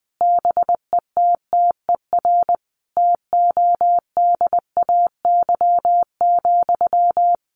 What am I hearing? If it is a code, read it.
Morse code, heard as a keyed tone, about 19 wpm, 700 Hz. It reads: BETTER TODAY,
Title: Sentences from Top 600 Words 20wpm